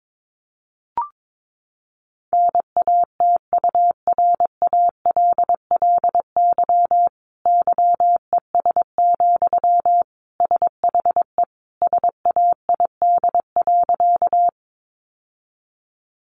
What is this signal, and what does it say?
Morse code, heard as a keyed tone, about 22 wpm, 700 Hz. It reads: NATURALLY YES, SHE SAID.
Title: Sentences from Top 1300 Words 22wpm